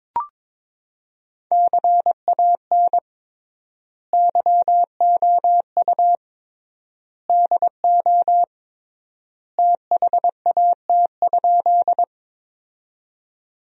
Morse code, heard as a keyed tone, about 22 wpm, 700 Hz. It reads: CAN YOU DO THAT?
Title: 3x Word-Spacing - Sentences from Top 100 Words - 22wpm